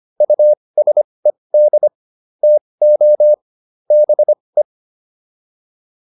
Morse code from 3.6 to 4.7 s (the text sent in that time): BE